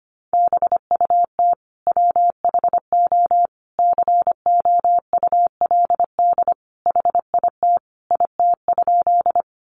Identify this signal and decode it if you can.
Morse code, heard as a keyed tone, about 25 wpm, 700 Hz. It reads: BUT WHO COULD HIT IT?
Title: Sentences from Top 1200 Words 25wpm